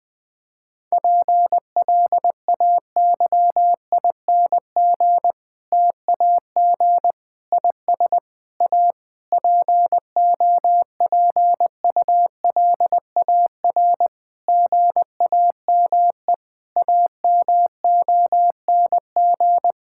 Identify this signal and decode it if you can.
Morse code, heard as a keyed tone, about 20 wpm, 700 Hz. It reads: PLAYING TAG IS A POPULAR GAME AMONG
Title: Difficult Sentences with Hints - 20wpm